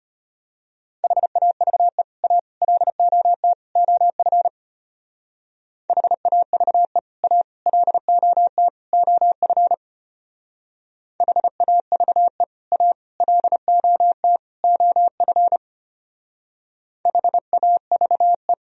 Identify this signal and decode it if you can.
Morse code, heard as a keyed tone, about 30 wpm, 700 Hz. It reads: HAVE A LOT OF HAVE A LOT OF HAVE A LOT OF HAVE
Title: Speed-Racing - Sets of 4 Words 25wpm